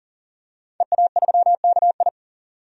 Morse code, heard as a keyed tone, about 40 wpm, 700 Hz. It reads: EA3KI
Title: DXCC-100 Call Signs - 5 Characters - 40wpm